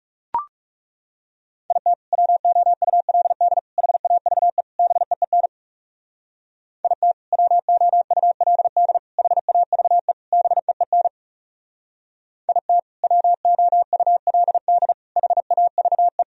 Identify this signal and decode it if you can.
Morse code, heard as a keyed tone, about 36 wpm, 700 Hz. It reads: IT WOULD HAVE BEEN IT WOULD HAVE BEEN IT WOULD HAVE
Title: Speed-Racing - Sets of 4 Words 30wpm